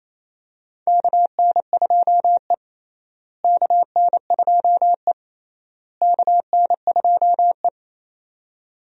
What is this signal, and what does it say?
Morse code, heard as a keyed tone, about 27 wpm, 700 Hz. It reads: KN2E KN2E KN2E
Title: USA Call Signs - 4 Characters - Repeated 3x - 28wpm